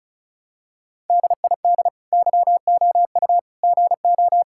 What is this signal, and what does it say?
Morse code, heard as a keyed tone, about 36 wpm, 700 Hz. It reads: DID YOU GO
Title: Sentences from Top 1000 Words - Review Part 5 - 35wpm